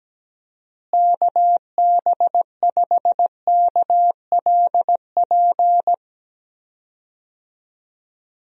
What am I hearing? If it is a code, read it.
Morse code, heard as a keyed tone, about 17 wpm, 700 Hz. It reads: KB5KLP
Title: USA Call Signs - 6 Characters - Repeated 2x - 17wpm